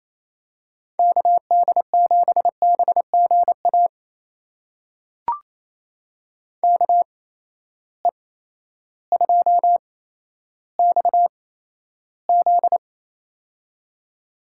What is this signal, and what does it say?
Morse code, heard as a keyed tone, about 28 wpm, spaced as for 13 wpm, 700 Hz. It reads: KD7BGA KE2XZ
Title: USA Call Signs - 8x Character Spacing - 28wpm